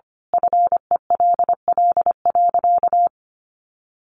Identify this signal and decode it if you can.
Morse code, heard as a keyed tone, about 25 wpm, 700 Hz. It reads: FELL.
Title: A Sound Of Thunder 25wpm